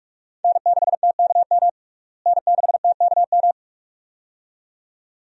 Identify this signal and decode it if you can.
Morse code, heard as a keyed tone, about 46 wpm, 700 Hz. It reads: N6TKM N6TKM
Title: USA Call Signs - 5 Characters - Repeated 2x - 45wpm